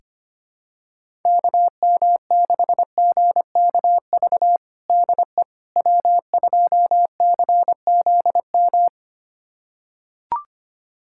Morse code, heard as a keyed tone, about 25 wpm, 700 Hz. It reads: KM6GKV DE W2CZM
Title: USA Callsign DE Callsign 25wpm